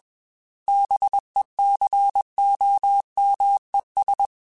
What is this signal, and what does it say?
Morse code, heard as a keyed tone, about 21 wpm, 800 Hz. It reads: BECOMES
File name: Technical Subjects / Technical Subjects-mp3s / AND Gate @20